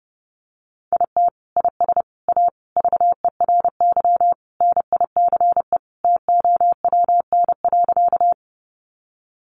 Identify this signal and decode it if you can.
Morse code, heard as a keyed tone, about 30 wpm, 700 Hz. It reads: IT IS A VERY NICE TOWN.
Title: Sentences from Top 1100 Words 30wpm